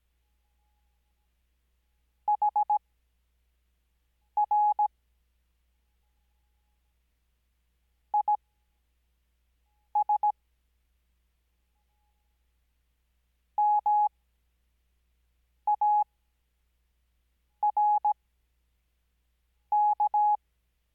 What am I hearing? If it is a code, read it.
Morse code, heard as a keyed tone, about 17 wpm, spaced as for 5 wpm, 840 Hz. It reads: HR IS MARK
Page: Farnsworth Spacing: 15WPM Character speed Spaced at 5WPM.